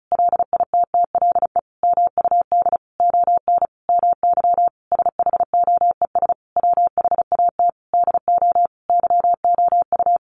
Morse code, heard as a keyed tone, about 34 wpm, 700 Hz. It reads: LITTLE MUD ON MY SHOES WHAT DO YOU
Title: A Sound Of Thunder 35wpm